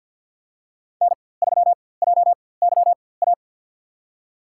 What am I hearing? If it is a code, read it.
Morse code, heard as a keyed tone, about 53 wpm, spaced as for 24 wpm, 700 Hz. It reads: N3JYA
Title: USA Call Signs - 5 Characters - 4x Character Spacing - 50wpm